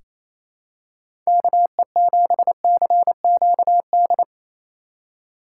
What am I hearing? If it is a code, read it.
Morse code, heard as a keyed tone, about 28 wpm, 700 Hz. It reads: KE7CQD
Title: USA Call Signs - 6x Character Spacing - 28wpm